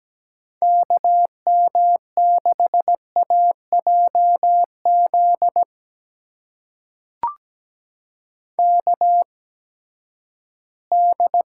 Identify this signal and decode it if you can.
Morse code, heard as a keyed tone, about 17 wpm, spaced as for 12 wpm, 700 Hz. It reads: KM6AJZ KD
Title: USA Call Signs - 6 Characters - 8x Character Spacing - 17wpm